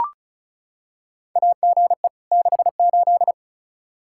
Morse code, heard as a keyed tone, about 36 wpm, 700 Hz. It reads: AGE 68
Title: Practice QSO - Exchange 3 - More Info - 35wpm